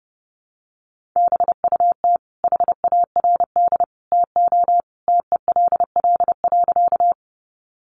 Morse code, heard as a keyed tone, about 30 wpm, 700 Hz. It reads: BUT HARD TO TELL.
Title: Sentences from Top 300 Words 30wpm